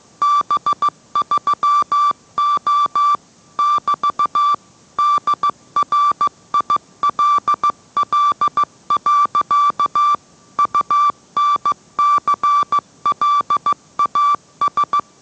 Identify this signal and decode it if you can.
Morse code, heard as a keyed tone, about 15 wpm, 1160 Hz. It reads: B3O <BT> DRILL. UNCLAS